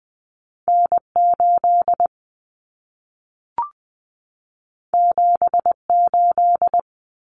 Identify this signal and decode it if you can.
Morse code, heard as a keyed tone, about 20 wpm, 700 Hz. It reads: N8 78
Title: Common Cut Numbers - 2-Digit 20wpm